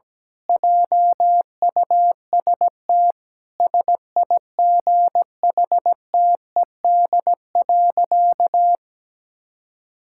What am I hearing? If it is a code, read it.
Morse code, heard as a keyed tone, about 17 wpm, 700 Hz. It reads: JUST SIGHTED.
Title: Pangram Challenge v2 - 17wpm